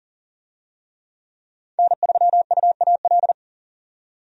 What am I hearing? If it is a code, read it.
Morse code, heard as a keyed tone, about 40 wpm, 700 Hz. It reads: N3UAL
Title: USA Call Signs - 5 Characters - 40wpm